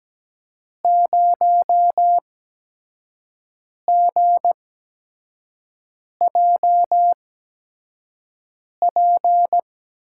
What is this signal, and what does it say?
Morse code, heard as a keyed tone, about 17 wpm, spaced as for 5 wpm, 700 Hz. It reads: 0GJP
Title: USA Call Signs - 8x Character Spacing - 17wpm